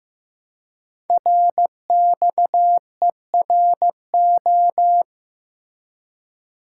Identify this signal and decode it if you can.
Morse code, heard as a keyed tone, about 15 wpm, 700 Hz. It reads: RXERO
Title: Code Groups - 5 Letters - 15wpm